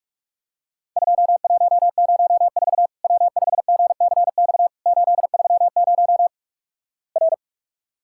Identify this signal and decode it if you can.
Morse code, heard as a keyed tone, about 46 wpm, 700 Hz. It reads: 2104 W5GCX 830 R
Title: Worked All Europe DX Contest - QTC Exchange - 45wpm